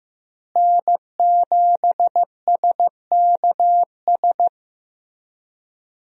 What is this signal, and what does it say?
Morse code, heard as a keyed tone, about 15 wpm, 700 Hz. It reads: N7SKS
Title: USA Call Signs - 5 Characters - Repeated 2x - 15wpm